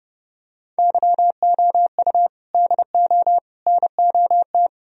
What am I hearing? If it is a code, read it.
Morse code, heard as a keyed tone, about 30 wpm, 700 Hz. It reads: YOU DO NOT